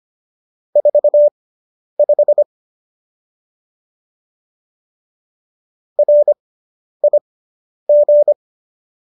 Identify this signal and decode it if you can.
Morse code, heard as a keyed tone, about 25 wpm, spaced as for 10 wpm, 590 Hz. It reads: 45 RIG